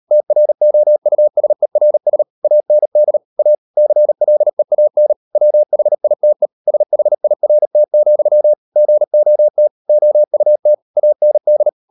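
Morse code, heard as a keyed tone, about 38 wpm, 600 Hz. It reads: TROUSERS AND A CLEAN WHITE SHIRT, GOT OUT AND
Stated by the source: Morse code practice at 38 WPM, sending Ken Follett's novel Never in English, from CW med Gnister.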